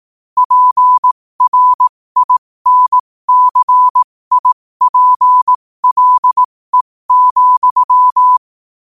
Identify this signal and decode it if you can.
Morse code, heard as a keyed tone, about 18 wpm, spaced as for 15 wpm, 1000 Hz. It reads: PRINCIPLE,
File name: 15wpm